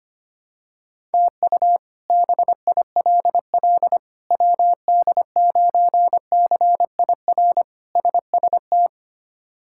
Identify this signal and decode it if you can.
Morse code, heard as a keyed tone, about 25 wpm, 700 Hz. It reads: TU BILL WD9CIR SST
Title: K1USN SST QSO Practice - 25wpm